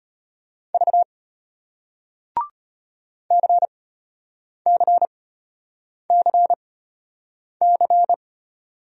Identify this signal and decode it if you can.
Morse code, heard as a keyed tone, about 33 wpm, spaced as for 8 wpm, 700 Hz. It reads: V CCCC